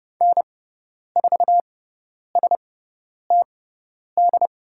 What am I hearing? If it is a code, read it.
Morse code, heard as a keyed tone, about 30 wpm, spaced as for 10 wpm, 690 Hz. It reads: N4STD
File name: Lesson 17g - Letter u - Call Signs 30wpm